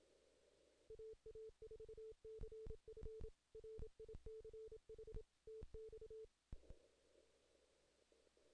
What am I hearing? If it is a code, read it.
Morse code, heard as a keyed tone, about 27 wpm, 440 Hz. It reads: AA4CF RICH TX
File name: 2016 JAN CW NAQP / 2016 Jan naqp k5ot - 0-10